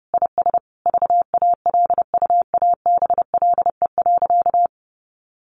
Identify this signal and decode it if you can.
Morse code, heard as a keyed tone, about 30 wpm, 700 Hz. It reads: IS VALUABLE.
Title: Sentences from Top 1100 Words 30wpm